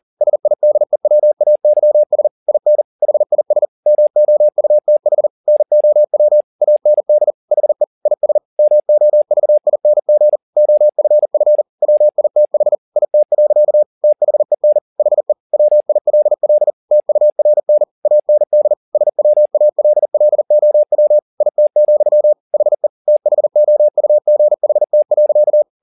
Morse code, heard as a keyed tone, about 40 wpm, 600 Hz. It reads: SIDEWAYS IN HIS MOUTH NOW AND HE IS MOVING OFF WITH IT. THEN HE WILL TURN AND SWALLOW IT, HE THOUGHT.